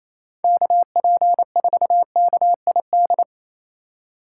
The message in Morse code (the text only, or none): KP4KID